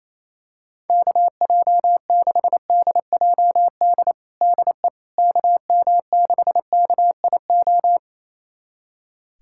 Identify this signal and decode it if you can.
Morse code, heard as a keyed tone, about 27 wpm, 700 Hz. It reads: KJ6DJD DE KM6KIO
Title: USA Callsign DE Callsign 28wpm